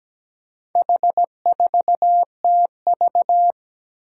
Morse code, heard as a keyed tone, about 17 wpm, 700 Hz. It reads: H4TV
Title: Four Challenging Letter-Number Combinations 17wpm